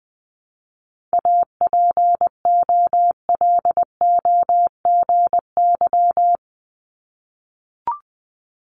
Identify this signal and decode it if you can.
Morse code, heard as a keyed tone, about 20 wpm, 700 Hz. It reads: APOLOGY
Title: Multi-letter Phonemes 20wpm